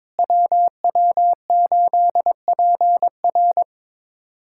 Morse code, heard as a keyed tone, about 22 wpm, 700 Hz. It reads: WW8PR
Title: USA Callsign DE Callsign - 5 Characters - 22wpm